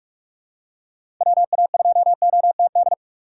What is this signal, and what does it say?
Morse code, heard as a keyed tone, about 46 wpm, 700 Hz. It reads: WA2OTD
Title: USA Call Signs - 4x Character Spacing - 45wpm